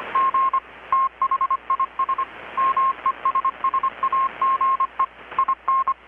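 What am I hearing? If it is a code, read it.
Morse code, heard as a keyed tone, about 25 wpm, 1070 Hz. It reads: G THIS MESSAGE IN
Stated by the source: Morse from KPH's Night of Nights XX maritime radiotelegraphy event, received in Chiapas, Mexico about 3,800 km away on a Kenwood R-600 with a grounded 10 m long-wire antenna.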